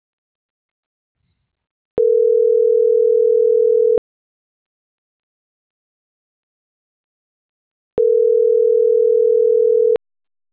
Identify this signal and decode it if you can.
Ringback tone (North America), 440 + 480 Hz.